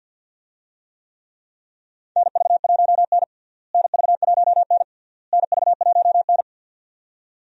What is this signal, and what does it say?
Morse code, heard as a keyed tone, about 52 wpm, 700 Hz. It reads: NV1N NV1N NV1N
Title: USA Call Signs - 4 Characters - Repeated 3x - 50wpm